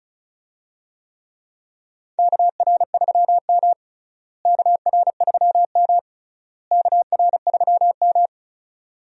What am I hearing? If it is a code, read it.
Morse code, heard as a keyed tone, about 34 wpm, 700 Hz. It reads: KR3M KR3M KR3M
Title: USA Call Signs - 4 Characters - Repeated 3x - 35wpm